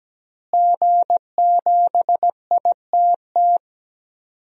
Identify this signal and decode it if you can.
Morse code, heard as a keyed tone, about 17 wpm, 700 Hz. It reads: G7ITT
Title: DXCC-100 Call Signs - 5 Characters - Repeated 2x - 17wpm